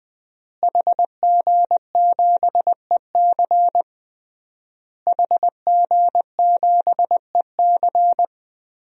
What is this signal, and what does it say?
Morse code, heard as a keyed tone, about 20 wpm, 700 Hz. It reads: HG7EC HG7EC
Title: DXCC-100 Call Signs - 5 Characters - Repeated 3x - 20wpm